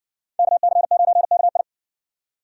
Morse code, heard as a keyed tone, about 57 wpm, 700 Hz. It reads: KC9KI